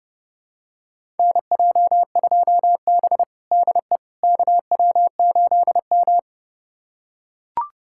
Morse code, heard as a keyed tone, about 30 wpm, 700 Hz. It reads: NJ2B DE KW8M
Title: USA Callsign DE Callsign - 4 Characters - 30wpm